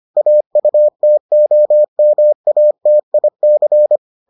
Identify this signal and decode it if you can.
Morse code, heard as a keyed tone, about 25 wpm, 600 Hz. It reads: AUTOMATIC